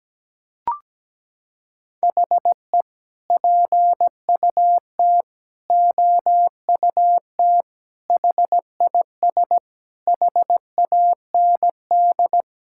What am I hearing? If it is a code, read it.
Morse code, heard as a keyed tone, about 17 wpm, 700 Hz. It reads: HE PUT OUT HIS HAND
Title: Sentences from Top 1000 Words - Review Part 5 - 17wpm